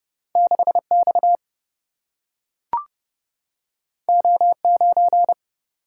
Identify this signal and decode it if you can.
Morse code, heard as a keyed tone, about 30 wpm, 700 Hz. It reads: 6 X O 9
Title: Single-Letter Single-Number 30wpm